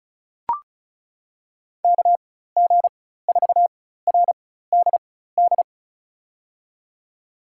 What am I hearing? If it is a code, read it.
Morse code, heard as a keyed tone, about 36 wpm, spaced as for 16 wpm, 700 Hz. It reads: KG4RDD
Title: USA Call Signs - 6 Characters - 4x Character Spacing - 35wpm